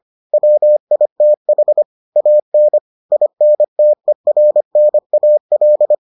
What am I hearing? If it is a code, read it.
Morse code, heard as a keyed tone, about 25 wpm, 600 Hz. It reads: WITH AN INTERNAL